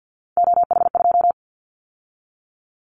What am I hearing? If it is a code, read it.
Morse code, heard as a keyed tone, about 50 wpm, 700 Hz. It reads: OH2
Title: Callsign Prefixes for Top 50 DXCC - 50wpm